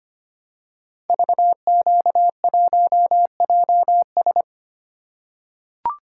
Morse code, heard as a keyed tone, about 25 wpm, 700 Hz. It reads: VQ1JS
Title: Code Groups - 5 Letter:Numbers - 25wpm